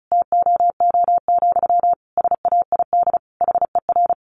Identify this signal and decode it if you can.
Morse code, heard as a keyed tone, about 34 wpm, 700 Hz. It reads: TOO, SAID HER
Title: Sentences from Top 700 Words 35wpm